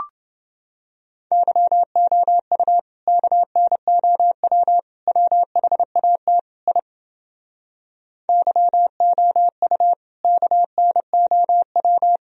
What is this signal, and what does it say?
Morse code, heard as a keyed tone, about 30 wpm, 700 Hz. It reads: YOU KNOW WHAT I YOU KNOW